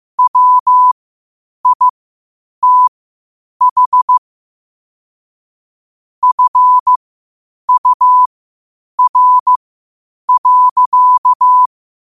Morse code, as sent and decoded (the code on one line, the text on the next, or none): .-- .. - .... / ..-. ..- .-. .-.-.-
WITH FUR.